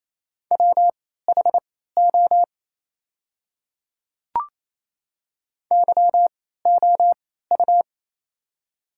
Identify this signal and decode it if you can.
Morse code, heard as a keyed tone, about 27 wpm, spaced as for 16 wpm, 690 Hz. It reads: WHO YOU
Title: 3x Character Spacing - Top 100 Words - 28wpm